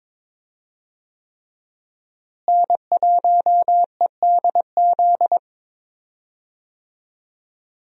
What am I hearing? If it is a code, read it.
Morse code, heard as a keyed tone, about 21 wpm, 700 Hz. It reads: N1EDZ
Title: USA Call Signs - 5 Characters - Repeated 3x - 22wpm